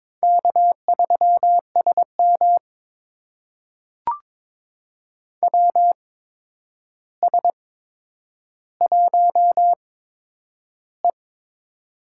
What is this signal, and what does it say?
Morse code, heard as a keyed tone, about 22 wpm, spaced as for 9 wpm, 700 Hz. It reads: K3SM WS1E
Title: USA Call Signs - 4 Characters - 8x Character Spacing - 22wpm